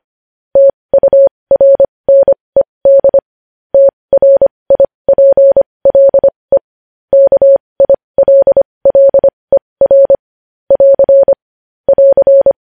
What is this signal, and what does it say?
Morse code, heard as a keyed tone, about 25 wpm, spaced as for 20 wpm, 550 Hz. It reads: TURNED TRIPLE KILLER <AR> <AR>
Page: News Headlines in Morse Code at 20 WPM